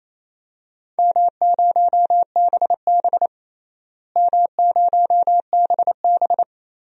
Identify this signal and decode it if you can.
Morse code, heard as a keyed tone, about 28 wpm, 700 Hz. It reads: M0BB M0BB
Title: DXCC-100 Call Signs - 4 Characters - Repeated 3x - 28wpm